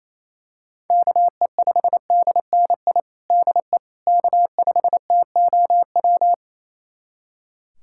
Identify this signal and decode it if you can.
Morse code, heard as a keyed tone, about 27 wpm, 700 Hz. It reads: KE5DNI DE K5TOW
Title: USA Callsign DE Callsign 28wpm